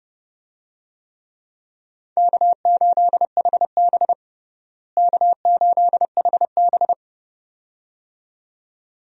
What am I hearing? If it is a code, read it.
Morse code, heard as a keyed tone, about 30 wpm, 700 Hz. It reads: K8HB K8HB
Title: USA Call Signs - 4 Characters - Repeated 2x - 30wpm